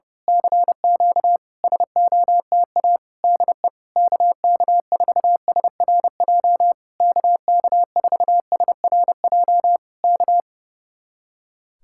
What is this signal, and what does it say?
Morse code, heard as a keyed tone, about 30 wpm, 700 Hz. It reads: CQ SOTA DE KK4SRJ KK4SRJ K
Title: Speed-Racing Common QSO Phrases - 30wpm